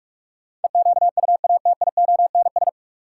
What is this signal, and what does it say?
Morse code, heard as a keyed tone, about 47 wpm, 700 Hz. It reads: EQUATIONS
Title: Sets of 5 Words - New Word Drill - 45wpm